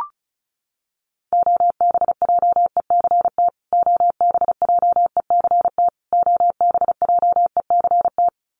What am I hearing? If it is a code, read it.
Morse code, heard as a keyed tone, about 34 wpm, 700 Hz. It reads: OBJECT OBJECT OBJECT
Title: Top 500 Words Repeated 35wpm